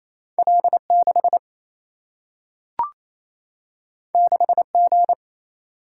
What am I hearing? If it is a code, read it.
Morse code, heard as a keyed tone, about 28 wpm, 700 Hz. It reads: L6 6G
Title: Single-Letter Single-Number 28wpm